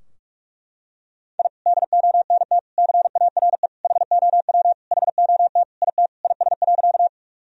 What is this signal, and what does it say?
Morse code, heard as a keyed tone, about 41 wpm, 700 Hz. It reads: I DONT CARE HOW HOT IT IS.
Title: Sentences from Top 1000 Words - Review Part 3 - 45wpm